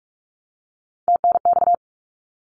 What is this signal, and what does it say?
Morse code, heard as a keyed tone, about 46 wpm, 700 Hz. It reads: TNX
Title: Abbreviations 45wpm